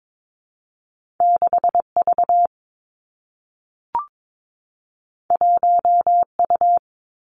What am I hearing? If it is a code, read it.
Morse code, heard as a keyed tone, about 22 wpm, 700 Hz. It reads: 6 V 1 U
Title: Two Challenging Letter-Number Combinations 22wpm